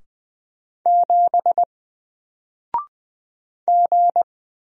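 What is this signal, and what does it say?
Morse code, heard as a keyed tone, about 20 wpm, 700 Hz. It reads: 7 G
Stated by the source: Single letter and number Morse drill at 20 wpm, from N4LCD.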